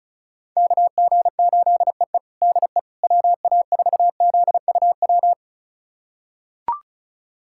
Morse code, heard as a keyed tone, about 36 wpm, 700 Hz. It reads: KG8EE DE WA4ZUW
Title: USA Callsign DE Callsign 35wpm